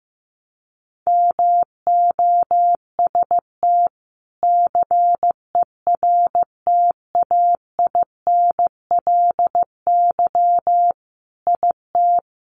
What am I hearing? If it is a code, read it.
Morse code, heard as a keyed tone, about 15 wpm, 700 Hz. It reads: MOST CERTAINLY IT
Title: Sentences from Top 900 Words 15wpm